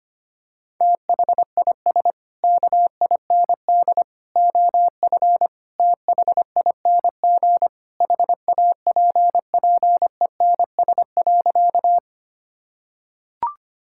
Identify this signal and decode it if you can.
Morse code, heard as a keyed tone, about 25 wpm, 700 Hz. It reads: THIS KIND OF THING HAPPENS.
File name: practice / 25wpm / Sentences with 3-Word Phrases - 25wpm